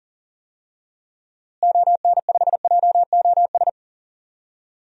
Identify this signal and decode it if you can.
Morse code, heard as a keyed tone, about 39 wpm, 700 Hz. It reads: ON5JOS
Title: DXCC-100 Call Signs - 6 Characters - Repeated 2x - 40wpm